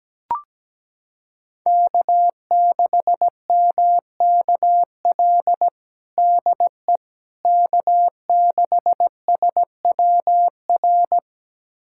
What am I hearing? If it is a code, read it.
Morse code, heard as a keyed tone, about 17 wpm, 700 Hz. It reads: K6MKL DE K6SWR
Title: USA Callsign DE Callsign - 5 Characters - 17wpm